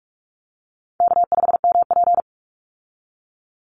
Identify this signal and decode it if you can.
Morse code, heard as a keyed tone, about 44 wpm, 700 Hz. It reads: K5MP
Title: Call Signs Repeated 2x - 45wpm